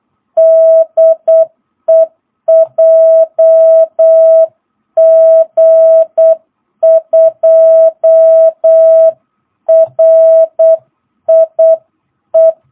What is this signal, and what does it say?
Morse code, heard as a keyed tone, about 8 wpm, 650 Hz. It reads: DE JG2RIE